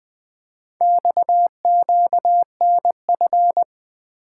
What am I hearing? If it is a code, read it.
Morse code, heard as a keyed tone, about 20 wpm, 700 Hz. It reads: XQNF
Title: Four Letter-Number Combinations - 20wpm